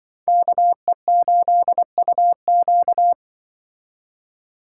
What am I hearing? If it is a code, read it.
Morse code, heard as a keyed tone, about 25 wpm, 700 Hz. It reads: KE8UQ